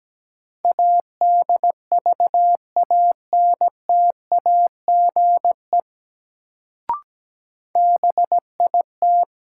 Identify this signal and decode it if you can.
Morse code, heard as a keyed tone, about 17 wpm, 700 Hz. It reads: ADVANTAGE BIT
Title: Sentences from Top 900 Words - New Word Drill - 17wpm